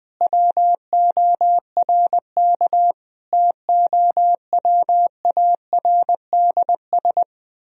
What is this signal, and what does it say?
Morse code, heard as a keyed tone, about 20 wpm, 700 Hz. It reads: WORK TOWARDS
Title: Sentences from Top 1000 Words - Review Part 3 - 20wpm